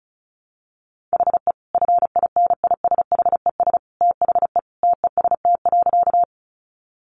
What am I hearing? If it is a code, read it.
Morse code, heard as a keyed tone, about 34 wpm, 700 Hz. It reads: HE FINISHES THE TEST.
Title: Additional Easy Phrases 35wpm